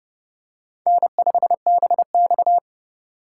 Morse code, heard as a keyed tone, about 30 wpm, 700 Hz. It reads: N5BX
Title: USA Call Signs - 4 Characters - 4x Character Spacing - 30wpm